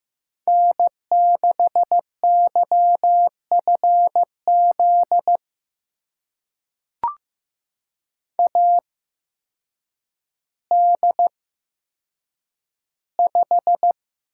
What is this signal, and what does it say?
Morse code, heard as a keyed tone, about 15 wpm, spaced as for 8 wpm, 700 Hz. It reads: N6YFZ AD5
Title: USA Call Signs - 8x Character Spacing - 15wpm